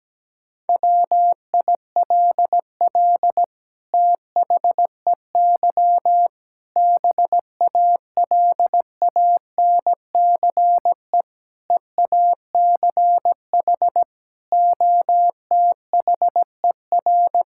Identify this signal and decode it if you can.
Morse code, heard as a keyed tone, about 17 wpm, 700 Hz. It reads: WILL THEY BALANCE EACH OTHER
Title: Sentences from Top 1000 Words - Review Part 6 - 17wpm